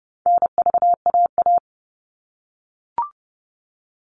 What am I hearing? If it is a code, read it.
Morse code, heard as a keyed tone, about 30 wpm, 700 Hz. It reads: NVAA
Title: All Cut Numbers - 4-Digit 30wpm